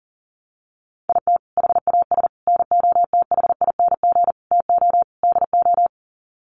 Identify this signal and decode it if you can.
Morse code, heard as a keyed tone, about 40 wpm, 700 Hz. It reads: IT HAS NOTHING TO DO
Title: Sets of 5 Words Encore 40wpm